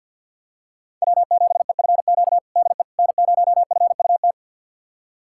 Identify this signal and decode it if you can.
Morse code, heard as a keyed tone, about 52 wpm, 700 Hz. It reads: W7EFQ DE N0FUT
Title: USA Callsign DE Callsign - 5 Characters - 50wpm